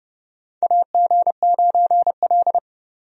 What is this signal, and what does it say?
Morse code, heard as a keyed tone, about 30 wpm, 700 Hz. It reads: AG9L
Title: USA Call Signs - 4 Characters - 30wpm